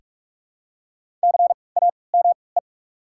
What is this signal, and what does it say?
Morse code, heard as a keyed tone, about 43 wpm, 700 Hz. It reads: C A M E
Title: 3x Character Spacing - Top 500 Words - 45wpm